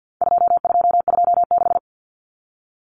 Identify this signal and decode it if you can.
Morse code, heard as a keyed tone, about 50 wpm, 700 Hz. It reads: 2226
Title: Common Cut Numbers - 4-Digit 50wpm